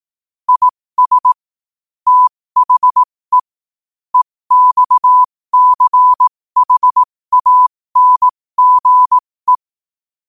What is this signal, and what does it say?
Morse code, heard as a keyed tone, about 18 wpm, spaced as for 15 wpm, 1000 Hz. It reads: IS THE EXCHANGE